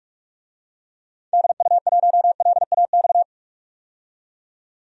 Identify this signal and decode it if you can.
Morse code, heard as a keyed tone, about 46 wpm, 700 Hz. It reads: DU1LAX
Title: DXCC-100 Call Signs - 6 Characters - 45wpm